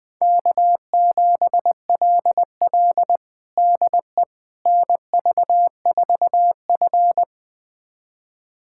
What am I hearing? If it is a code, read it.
Morse code, heard as a keyed tone, about 20 wpm, 700 Hz. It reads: K7LL DE NV4F
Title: USA Callsign DE Callsign - 4 Characters - 20wpm